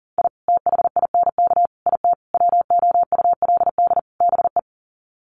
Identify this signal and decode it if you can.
Morse code, heard as a keyed tone, about 40 wpm, 700 Hz. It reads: I THINK IT WOULD BE
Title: Sets of 5 Words 40wpm